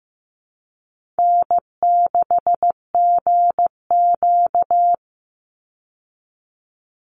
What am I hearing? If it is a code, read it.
Morse code, heard as a keyed tone, about 15 wpm, 700 Hz. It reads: N6GQ
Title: Easy Call Signs Repeated 3x - 15wpm